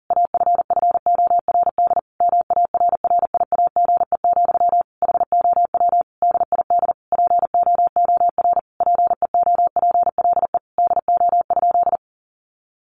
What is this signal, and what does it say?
Morse code, heard as a keyed tone, about 40 wpm, 700 Hz. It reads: AFFORD MARRIAGE, HOW DID POOR PEOPLE DO?
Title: Sentences from Top 1100 Words 40wpm